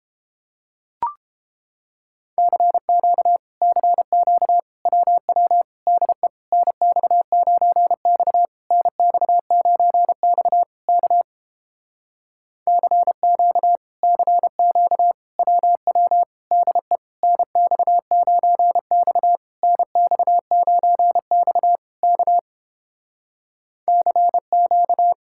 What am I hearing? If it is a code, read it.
Morse code, heard as a keyed tone, about 31 wpm, 700 Hz. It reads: CQ CQ WW DE NX9X NX9X K CQ CQ WW DE NX9X NX9X K CQ